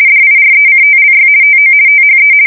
RTTY, about 50 baud. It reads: DIRECTLY TO THE
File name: fsk170-50B / teletype-corp